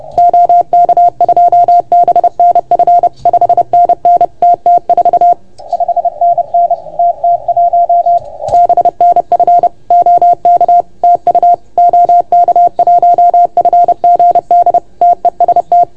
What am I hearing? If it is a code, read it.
Morse code, heard as a keyed tone, about 30 wpm, 690 Hz. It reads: OK2BNF 5NN TT4 5NN TT1 BNF OK TU OK1FGD TEST
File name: Archiv / CUC381 / OK1FGD CUC 381